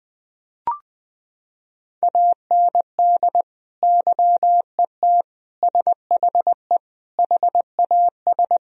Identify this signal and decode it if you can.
Morse code, heard as a keyed tone, about 21 wpm, 700 Hz. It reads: AND YET SHE HAS
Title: Sentences from Top 1000 Words - Review Part 1 - 20wpm